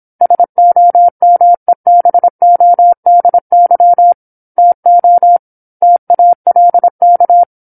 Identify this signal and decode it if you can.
Morse code, heard as a keyed tone, about 27 wpm, 700 Hz. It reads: SOMEBODY TO TALK
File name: 08-Chapter-08 26 WPM